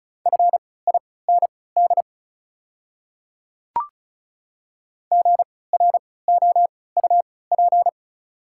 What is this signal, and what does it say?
Morse code, heard as a keyed tone, about 36 wpm, spaced as for 20 wpm, 700 Hz. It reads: FIND GROUP
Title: 3x Character Spacing - Top 300 Words - 35wpm